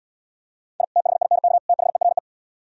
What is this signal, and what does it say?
Morse code, heard as a keyed tone, about 76 wpm, 700 Hz. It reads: I THINK THERE